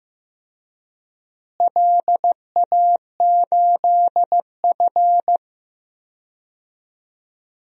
Morse code, heard as a keyed tone, about 15 wpm, 700 Hz. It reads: LA8F
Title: DXCC-100 Call Signs - 4 Characters - Repeated 3x - 15wpm